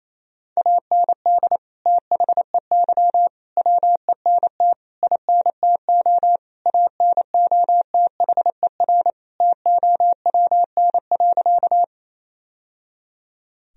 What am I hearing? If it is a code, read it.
Morse code, heard as a keyed tone, about 28 wpm, 700 Hz. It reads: AND THEY WENT INTO ANOTHER TOWN.
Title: Sentences from Top 1000 Words - Review Part 2 - 28wpm